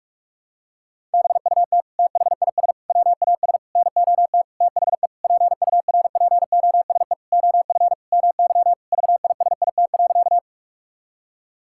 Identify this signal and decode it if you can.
Morse code, heard as a keyed tone, about 46 wpm, 700 Hz. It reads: BUT THIS WAS NOT THE PURPOSE OF MY VISIT.
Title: Sentences from Top 1000 Words - Review Part 6 - 45wpm